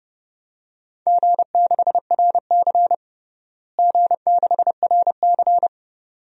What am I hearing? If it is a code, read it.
Morse code, heard as a keyed tone, about 30 wpm, 700 Hz. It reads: G6RC G6RC
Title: DXCC-100 Call Signs - 4 Characters - Repeated 2x - 30wpm